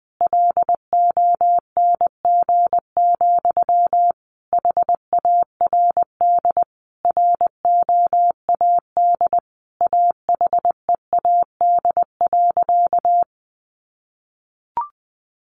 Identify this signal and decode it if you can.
Morse code, heard as a keyed tone, about 20 wpm, 700 Hz. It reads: LONG, HARD ROAD AHEAD.
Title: Sentences from Top 1100 Words 20wpm